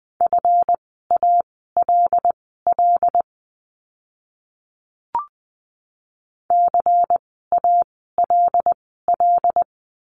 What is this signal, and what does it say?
Morse code, heard as a keyed tone, about 20 wpm, spaced as for 15 wpm, 700 Hz. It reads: FALL CALL
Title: 139 Lesson 23f - Letter f - Words 20wpm